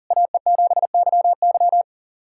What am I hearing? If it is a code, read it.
Morse code, heard as a keyed tone, about 44 wpm, 690 Hz. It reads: AE7YY
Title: SOTA QSO Practice 40wpm